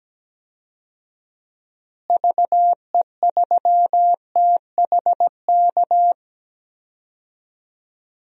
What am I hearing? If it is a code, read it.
Morse code, heard as a keyed tone, about 17 wpm, 700 Hz. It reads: VE3THK
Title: DXCC-100 Call Signs - 6 Characters - Repeated 3x - 17wpm